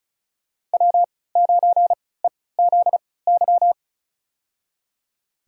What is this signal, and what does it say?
Morse code, heard as a keyed tone, about 34 wpm, spaced as for 20 wpm, 700 Hz. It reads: W9EZY
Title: USA Call Signs - 5 Characters - 3x Character Spacing - 35wpm